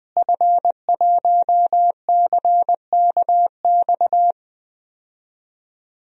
Morse code, heard as a keyed tone, about 20 wpm, 700 Hz. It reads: F1CKX
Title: DXCC-100 Call Signs - 5 Characters - Repeated 2x - 20wpm